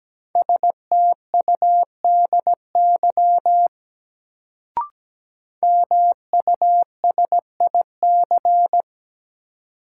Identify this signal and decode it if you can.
Morse code, heard as a keyed tone, about 17 wpm, 700 Hz. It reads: STUDY MUSIC
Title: Sentences from Top 300 Words - New Word Drill - 17wpm